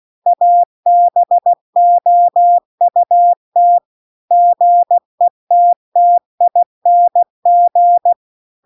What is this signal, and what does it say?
Morse code, heard as a keyed tone, about 16 wpm, 700 Hz. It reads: ABOUT GETTING
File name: CW-The-Big-Twoheaded-River-2-16WPM